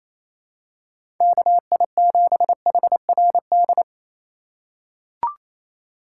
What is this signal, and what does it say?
Morse code, heard as a keyed tone, about 28 wpm, 700 Hz. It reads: KI7HRD
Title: USA Call Signs - 6 Characters - 6x Character Spacing - 28wpm